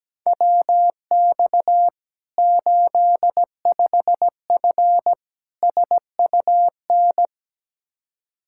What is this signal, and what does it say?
Morse code, heard as a keyed tone, about 17 wpm, 700 Hz. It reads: WX 85F SUN
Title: Practice QSO - Exchange 2 - Info - 17wpm